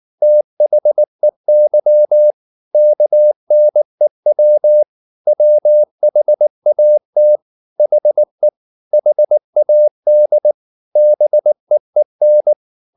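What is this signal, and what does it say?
Morse code, heard as a keyed tone, about 19 wpm, 600 Hz. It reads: THEY KNEW WHAT HE HAD BEEN
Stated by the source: Morse code at 19 wpm from CW med Gnister, reading Ken Follett's novel Never in English.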